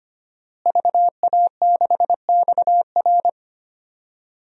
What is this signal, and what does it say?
Morse code, heard as a keyed tone, about 25 wpm, 700 Hz. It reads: VA6XR
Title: DXCC-100 Call Signs - 5 Characters - Repeated 3x - 25wpm